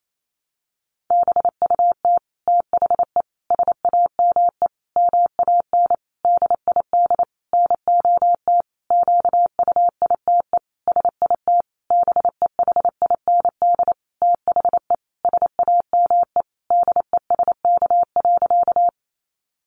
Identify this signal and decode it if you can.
Morse code, heard as a keyed tone, about 28 wpm, 700 Hz. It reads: BUT THE SAME MAN DID NOT QUITE SIT BEHIND THE SAME DESK.
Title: A Sound Of Thunder 28wpm